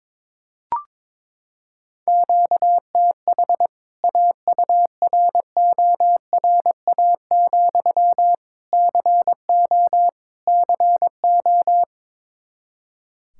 Morse code, heard as a keyed tone, about 22 wpm, 700 Hz. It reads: QTH AURORA, CO CO
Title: K1USN SST QSO Practice 22wpm